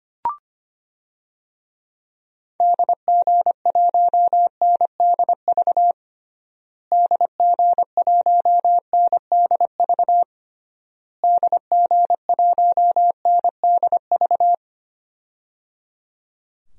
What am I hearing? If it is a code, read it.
Morse code, heard as a keyed tone, about 25 wpm, 700 Hz. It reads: DG1NDV DG1NDV DG1NDV
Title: DXCC-100 Call Signs - 6 Characters - Repeated 3x - 25wpm